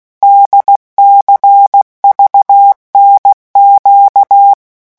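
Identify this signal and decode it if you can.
Morse code, heard as a keyed tone, about 16 wpm, 800 Hz. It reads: DCVNQ
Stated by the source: Morse letter-group revision drill at 15 words per minute.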